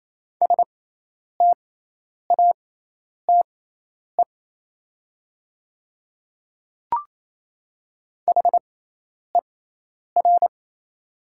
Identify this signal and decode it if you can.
Morse code, heard as a keyed tone, about 28 wpm, spaced as for 10 wpm, 690 Hz. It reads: STATE HER
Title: 6x Character Spacing - Top 500 Words - 28wpm